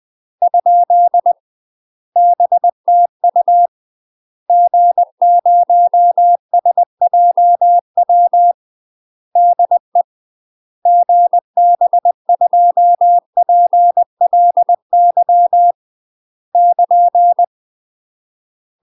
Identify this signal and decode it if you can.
Morse code, heard as a keyed tone, about 20 wpm, 700 Hz. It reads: ? BTU G0SJW DE GB2PLY <KN>
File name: UK MASTER QSOs TRAINING / 20WPM / UK MASTER QSOs 0030